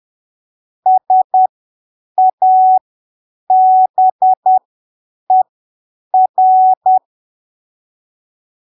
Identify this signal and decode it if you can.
Morse code, heard as a keyed tone, about 10 wpm, spaced as for 7 wpm, 750 Hz. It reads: SABER
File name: arquivos / texto / texto 21 10ppm